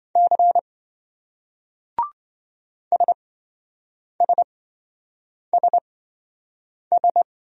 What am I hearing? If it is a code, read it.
Morse code, heard as a keyed tone, about 26 wpm, spaced as for 7 wpm, 700 Hz. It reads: C SSSS